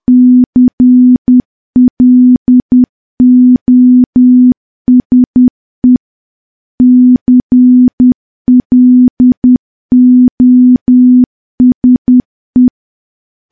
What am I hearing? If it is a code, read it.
Morse code, heard as a keyed tone, about 10 wpm, 250 Hz. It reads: CLOSE CLOSE
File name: morse / morse-training-10WPM